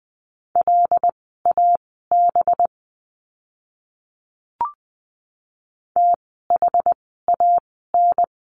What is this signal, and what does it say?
Morse code, heard as a keyed tone, about 20 wpm, spaced as for 15 wpm, 700 Hz. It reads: LAB THAN
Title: 185 Lesson 30f - Letter b - Words 20wpm